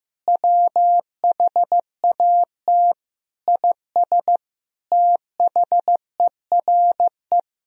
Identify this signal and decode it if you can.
Morse code, heard as a keyed tone, about 15 wpm, 700 Hz. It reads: WHAT IS THERE
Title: Sentences from Top 200 Words and Abbreviations 15wpm